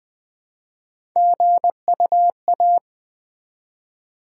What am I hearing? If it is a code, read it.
Morse code, heard as a keyed tone, about 20 wpm, 700 Hz. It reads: GUA
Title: Build A Word - 1 Letter At A Time - 5-Letter Words - 20wpm